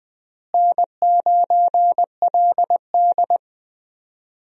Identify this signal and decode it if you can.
Morse code, heard as a keyed tone, about 20 wpm, 700 Hz. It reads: N9LD
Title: USA Call Signs - 4 Characters - Repeated 3x - 20wpm